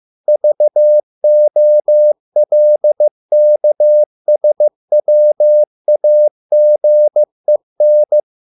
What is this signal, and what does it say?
Morse code, heard as a keyed tone, about 15 wpm, 600 Hz. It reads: VOLKSWAGEN